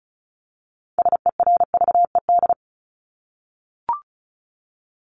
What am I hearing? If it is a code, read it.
Morse code, heard as a keyed tone, about 34 wpm, 700 Hz. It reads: SERVED
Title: ED Suffix - Top 100 Words with Base Words 35wpm